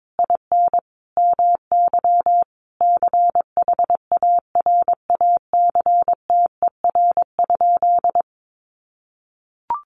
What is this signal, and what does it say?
Morse code, heard as a keyed tone, about 22 wpm, 700 Hz. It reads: IN MY CHARACTER?
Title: Sentences from Top 1200 Words 22wpm